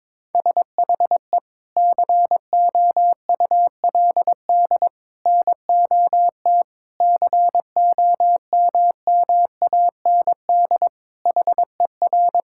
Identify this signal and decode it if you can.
Morse code, heard as a keyed tone, about 22 wpm, 700 Hz. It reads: SHE COULD NOT COMMAND HER
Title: Sentences from Top 1600 Words 22wpm